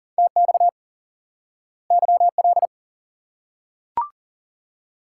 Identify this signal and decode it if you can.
Morse code, heard as a keyed tone, about 40 wpm, 700 Hz. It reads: TX YL
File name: IQE Course 3 - Lesson 89 - YAGI - Two Elements - 40wpm